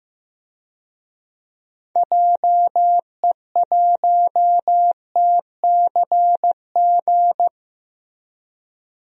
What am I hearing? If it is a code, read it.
Morse code, heard as a keyed tone, about 15 wpm, 700 Hz. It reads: JE1TCG
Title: DXCC-100 Call Signs - 6 Characters - Repeated 3x - 15wpm